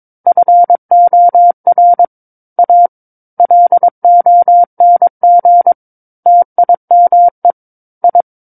Morse code, heard as a keyed tone, about 22 wpm, 700 Hz. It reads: FOR A LONG TIME I
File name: Book One / 13-Chapter 13 - 22 WPM